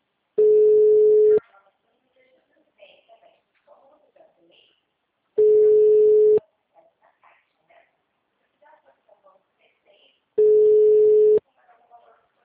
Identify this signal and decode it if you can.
Ringback tone (Europe, 425 Hz), with speech.
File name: out- -2011- - - .36